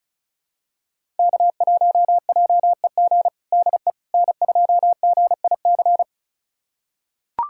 Morse code, heard as a keyed tone, about 34 wpm, 700 Hz. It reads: K1JEG DE N2GIC
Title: USA Callsign DE Callsign 35wpm